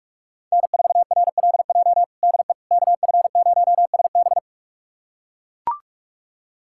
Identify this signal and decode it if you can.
Morse code, heard as a keyed tone, about 41 wpm, 700 Hz. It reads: N4RLJ DE KF0SB
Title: USA Callsign DE Callsign - 5 Characters - 45wpm